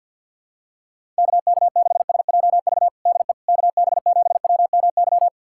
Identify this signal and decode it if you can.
Morse code, heard as a keyed tone, about 51 wpm, 700 Hz. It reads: KK6SJV DE KB7WMY
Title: USA Callsign DE Callsign - 6 Characters - 50wpm